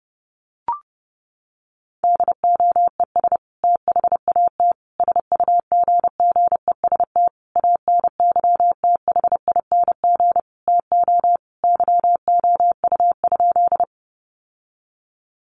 Morse code, heard as a keyed tone, about 30 wpm, 700 Hz. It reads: DOES THAT SUGGEST ANYTHING TO YOU?
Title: Sentences from Top 1300 Words 30wpm